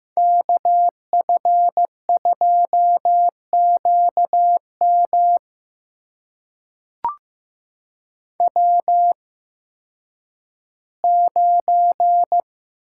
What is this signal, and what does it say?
Morse code, heard as a keyed tone, about 15 wpm, spaced as for 10 wpm, 700 Hz. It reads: KF2QM W9
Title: USA Call Signs - 5 Characters - 8x Character Spacing - 15wpm